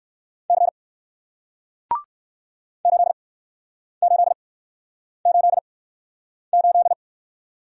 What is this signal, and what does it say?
Morse code, heard as a keyed tone, about 61 wpm, spaced as for 9 wpm, 700 Hz. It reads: C 8888